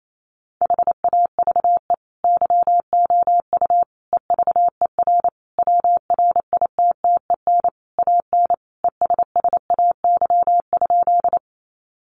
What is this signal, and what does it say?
Morse code, heard as a keyed tone, about 28 wpm, 700 Hz. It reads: HAVE YOU EVER WRITTEN AN ESSAY?
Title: Sentences from Top 1500 Words 28wpm